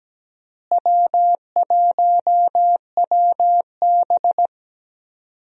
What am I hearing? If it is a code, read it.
Morse code, heard as a keyed tone, about 17 wpm, 700 Hz. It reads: W1WB
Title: USA Call Signs - 4 Characters - Repeated 3x - 17wpm